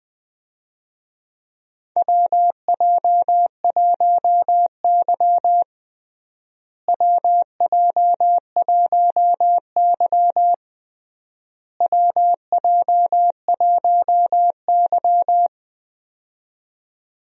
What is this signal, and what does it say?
Morse code, heard as a keyed tone, about 20 wpm, 700 Hz. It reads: WJ1Y WJ1Y WJ1Y
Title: USA Call Signs - 4 Characters - Repeated 3x - 20wpm